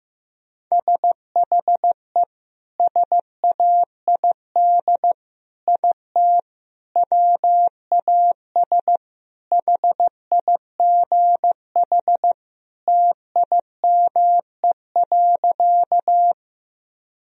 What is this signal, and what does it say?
Morse code, heard as a keyed tone, about 15 wpm, 700 Hz. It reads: SHE SAID IT WAS HIGH TIME.
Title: Sentences from Top 1000 Words - Review Part 1 - 15wpm